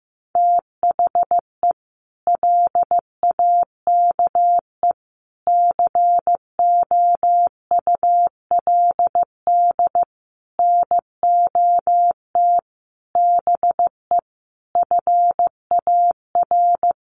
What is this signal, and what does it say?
Morse code, heard as a keyed tone, about 15 wpm, 700 Hz. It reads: THE LAKE COULD NOT BE FAR
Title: Sentences from Top 1100 Words 15wpm